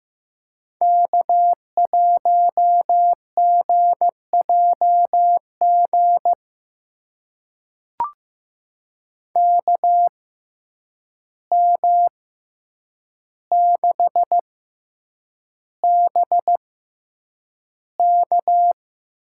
Morse code, heard as a keyed tone, about 15 wpm, spaced as for 8 wpm, 700 Hz. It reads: K1GJG KM6BK
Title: USA Call Signs - 5 Characters - 6x Character Spacing - 15wpm